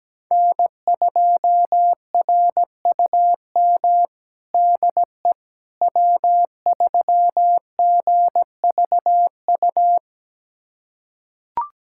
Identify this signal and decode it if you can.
Morse code, heard as a keyed tone, about 17 wpm, 700 Hz. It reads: N2RUM DE W3GVU
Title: USA Callsign DE Callsign - 5 Characters - 17wpm